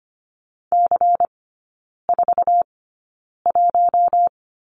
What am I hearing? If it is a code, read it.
Morse code, heard as a keyed tone, about 25 wpm, spaced as for 9 wpm, 700 Hz. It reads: C41
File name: course / 25:10wpm / 100 Lesson 17e - Letter u - Three Characters 25wpm